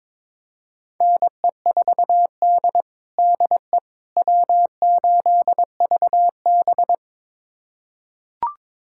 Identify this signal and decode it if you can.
Morse code, heard as a keyed tone, about 22 wpm, 700 Hz. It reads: NE4D DE W8VB
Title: USA Callsign DE Callsign - 4 Characters - 22wpm